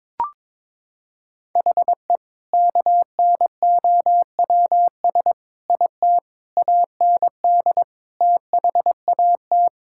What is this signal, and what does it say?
Morse code, heard as a keyed tone, about 22 wpm, 700 Hz. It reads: HE KNOWS IT AND THAT
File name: practice / 22wpm / Sentences with 2-Word Phrases - 22wpm